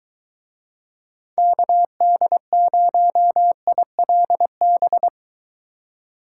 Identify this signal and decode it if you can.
Morse code, heard as a keyed tone, about 23 wpm, 700 Hz. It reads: KD0ILB